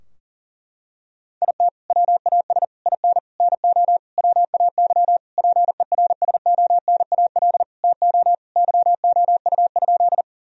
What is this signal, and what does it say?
Morse code, heard as a keyed tone, about 40 wpm, 700 Hz. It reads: IT WAS IN NO WAY PERSONAL TO YOU?
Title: Sentences from Top 1000 Words - Review Part 4 - 40wpm